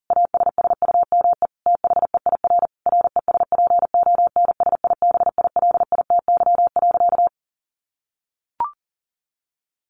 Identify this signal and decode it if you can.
Morse code, heard as a keyed tone, about 40 wpm, 700 Hz. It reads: ASSUME THEIR RESPONSIBILITY.
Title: Sentences from Top 1500 Words 40wpm